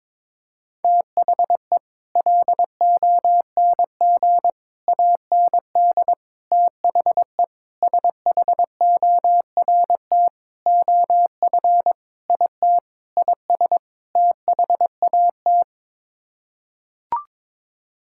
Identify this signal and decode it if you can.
Morse code, heard as a keyed tone, about 22 wpm, 700 Hz. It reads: THE LONG AND THE SHORT OF IT IS THAT
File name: Sets of 9 Words - 22wpm